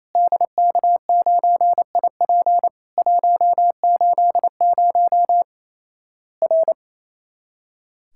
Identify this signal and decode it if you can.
Morse code, heard as a keyed tone, about 27 wpm, 700 Hz. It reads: DK9IP 180 R
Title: Worked All Europe DX Contest - QTC Exchange - 28wpm